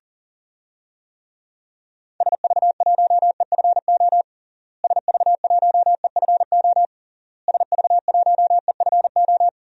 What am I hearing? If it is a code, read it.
Morse code, heard as a keyed tone, about 40 wpm, 700 Hz. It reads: SV1EFO SV1EFO SV1EFO
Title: DXCC-100 Call Signs - 6 Characters - Repeated 3x - 40wpm